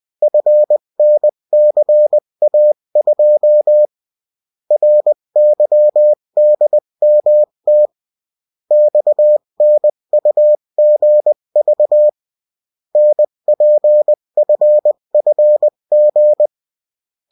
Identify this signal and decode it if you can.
Morse code, heard as a keyed tone, about 20 wpm, 600 Hz. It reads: FNCA2 RYDMT XNUGV NPFFG
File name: morse mp3 / revision9-vitesse 18 mots